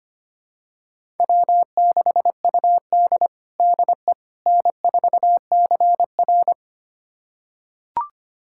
Morse code, heard as a keyed tone, about 25 wpm, 700 Hz. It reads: W6UD DE N4CR
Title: USA Callsign DE Callsign - 4 Characters - 25wpm